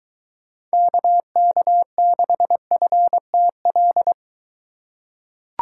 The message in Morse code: KK6FTL

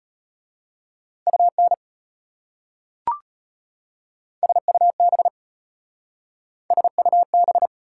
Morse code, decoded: UN SUB SUB

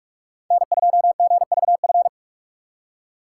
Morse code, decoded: N2GVF

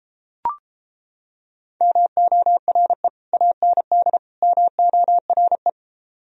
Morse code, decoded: MORE AND MORE

